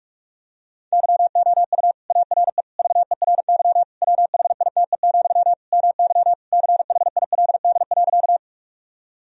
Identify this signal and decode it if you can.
Morse code, heard as a keyed tone, about 46 wpm, 700 Hz. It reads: YOU ARE VERY WHITE, MY CHILD.
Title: Sentences from Top 1000 Words - Review Part 1 - 45wpm